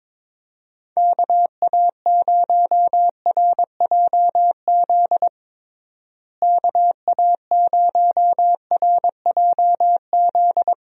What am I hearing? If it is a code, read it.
Morse code, heard as a keyed tone, about 22 wpm, 700 Hz. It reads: KA0RJZ KA0RJZ
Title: USA Call Signs - 6 Characters - Repeated 3x - 22wpm